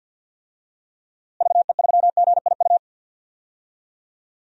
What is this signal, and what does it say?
Morse code, heard as a keyed tone, about 48 wpm, 700 Hz. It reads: VE3GIU
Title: DXCC-100 Call Signs - 6 Characters - Repeated 3x - 50wpm